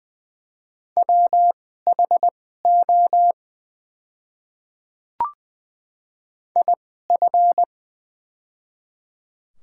Morse code, heard as a keyed tone, about 20 wpm, spaced as for 15 wpm, 700 Hz. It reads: WHO IF
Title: 2x Character Spacing - Top 100 Words - 20wpm